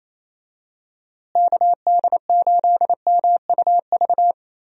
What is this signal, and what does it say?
Morse code, heard as a keyed tone, about 28 wpm, 700 Hz. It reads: KD8MUV
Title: USA Call Signs - 6 Characters - Repeated 3x - 28wpm